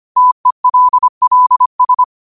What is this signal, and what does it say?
Morse code, heard as a keyed tone, about 25 wpm, 1000 Hz. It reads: TELLS